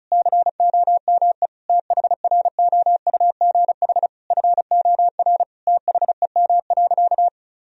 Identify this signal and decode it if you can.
Morse code, heard as a keyed tone, about 36 wpm, 700 Hz. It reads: COME THROUGH FOR THEM.
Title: Sentences from Top 1000 Words - Review Part 1 - 35wpm